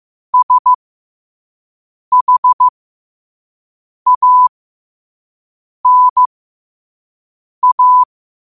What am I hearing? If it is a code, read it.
Morse code, heard as a keyed tone, about 15 wpm, spaced as for 5 wpm, 1000 Hz. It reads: SHANA
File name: qotdfiles / 5wpm